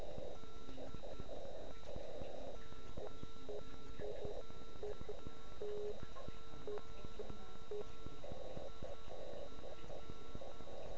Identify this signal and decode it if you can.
Morse code, heard as a keyed tone, about 9 wpm, 1340 Hz. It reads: DE OK1IF